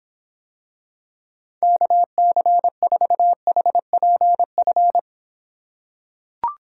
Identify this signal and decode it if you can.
Morse code, heard as a keyed tone, about 26 wpm, 700 Hz. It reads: KC4HPF